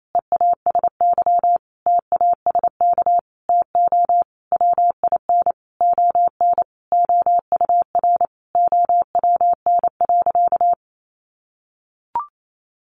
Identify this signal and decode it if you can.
Morse code, heard as a keyed tone, about 28 wpm, 700 Hz. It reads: EASY TASK TO WIN ON OUR OWN.
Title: Sentences from Top 1500 Words 28wpm